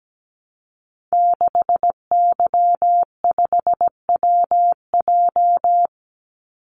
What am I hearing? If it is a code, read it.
Morse code, heard as a keyed tone, about 17 wpm, 700 Hz. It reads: 6Y5WJ
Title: Call Signs Repeated 3x - 17wpm